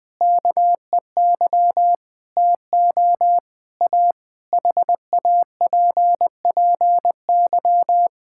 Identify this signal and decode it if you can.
Morse code, heard as a keyed tone, about 20 wpm, 700 Hz. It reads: KEY TO A HAPPY
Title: Difficult Sentences with Hints - 20wpm